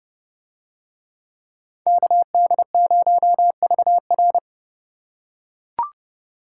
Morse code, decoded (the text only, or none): KD0VR